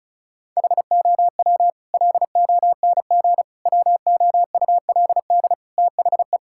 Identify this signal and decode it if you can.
Morse code, heard as a keyed tone, about 36 wpm, 700 Hz. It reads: HOW LONG WOULD THE
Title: Sentences from Top 1000 Words - Review Part 5 - 35wpm